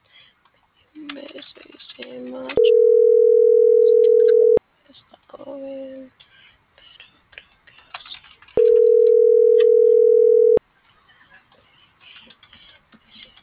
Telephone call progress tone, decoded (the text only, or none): ringback tone